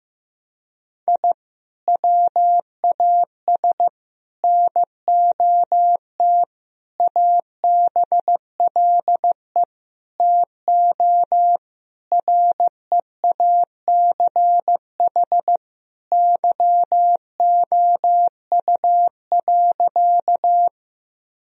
Morse code, heard as a keyed tone, about 16 wpm, 700 Hz. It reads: I WAS NOT ABLE TO REACH YOU.
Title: Sentences from Top 1000 Words - Review Part 2 - 15wpm